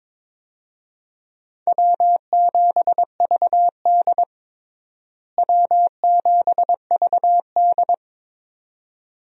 Morse code, heard as a keyed tone, about 22 wpm, 700 Hz. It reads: W7VD W7VD
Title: USA Call Signs - 4 Characters - Repeated 2x - 22wpm